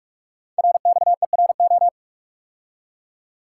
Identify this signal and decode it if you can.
Morse code, heard as a keyed tone, about 44 wpm, 700 Hz. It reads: RXERO
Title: Code Groups - 5 Letters - 45wpm